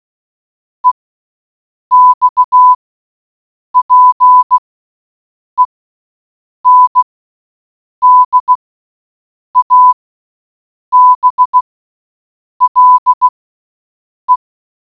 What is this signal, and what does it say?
Morse code, heard as a keyed tone, about 16 wpm, spaced as for 7 wpm, 1000 Hz. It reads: EXPENDABLE